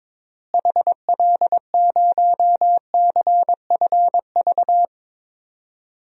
Morse code, heard as a keyed tone, about 21 wpm, 700 Hz. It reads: HL0CFV
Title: DXCC-100 Call Signs - 6 Characters - Repeated 2x - 22wpm